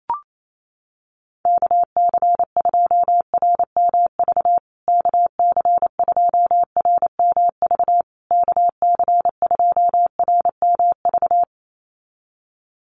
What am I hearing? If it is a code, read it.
Morse code, heard as a keyed tone, about 28 wpm, 700 Hz. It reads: KC2RMV KC2RMV KC2RMV
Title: USA Call Signs Repeated 3x 28wpm